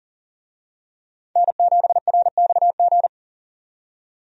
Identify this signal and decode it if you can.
Morse code, heard as a keyed tone, about 40 wpm, 700 Hz. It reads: N7RXG
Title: USA Call Signs - 5 Characters - 40wpm